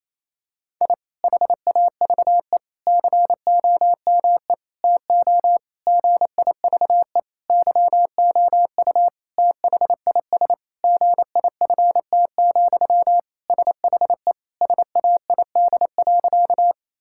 Morse code, heard as a keyed tone, about 28 wpm, 700 Hz. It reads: I HAVE COME TO GIVE YOU THIS GIFT, SHE SAID.
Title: Sentences from Top 1000 Words - Review Part 6 - 28wpm